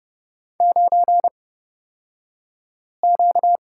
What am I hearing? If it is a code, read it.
Morse code, heard as a keyed tone, about 29 wpm, spaced as for 5 wpm, 700 Hz. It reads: 9Q
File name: Lesson 38d - Letter q - Two Characters 30wpm